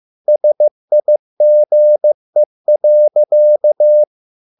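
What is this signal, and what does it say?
Morse code, heard as a keyed tone, about 15 wpm, 600 Hz. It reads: SIGE.
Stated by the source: Morse code at a slow 15 wpm from CW med Gnister.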